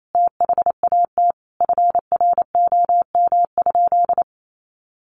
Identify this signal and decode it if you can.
Morse code, heard as a keyed tone, about 28 wpm, 700 Hz. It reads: THAT FROM?
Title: Sentences from Top 300 Words 28wpm